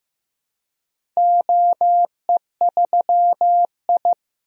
Morse code, heard as a keyed tone, about 15 wpm, 700 Hz. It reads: OE3I
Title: DXCC-100 Call Signs - 4 Characters - Repeated 2x - 15wpm